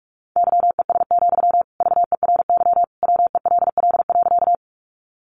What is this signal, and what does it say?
Morse code, heard as a keyed tone, about 44 wpm, 700 Hz. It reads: YES, VERY WELL.
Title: Sentences from Top 500 Words 45wpm